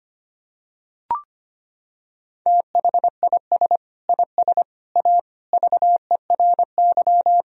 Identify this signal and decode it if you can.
Morse code, heard as a keyed tone, about 25 wpm, 700 Hz. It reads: THIS IS A VERY
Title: Sentences from Top 1000 Words - Review Part 3 - 25wpm